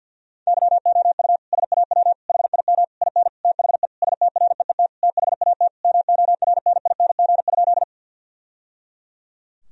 Morse code, heard as a keyed tone, about 50 wpm, 700 Hz. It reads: YOU SAW HIM IN THE STREET THAT MORNING?
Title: Sentences from Top 1000 Words - Review Part 3 - 50wpm